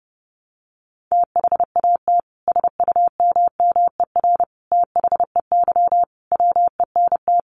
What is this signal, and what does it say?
Morse code, heard as a keyed tone, about 30 wpm, 700 Hz. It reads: THAT SUMMER THEY WENT
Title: Sentences from Top 1100 Words 30wpm